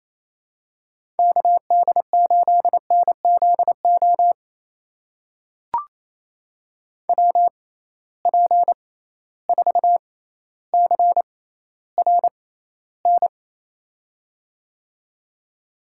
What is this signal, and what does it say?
Morse code, heard as a keyed tone, about 28 wpm, spaced as for 14 wpm, 700 Hz. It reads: KD8NZO WP4CRN
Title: USA Call Signs - 6 Characters - 6x Character Spacing - 28wpm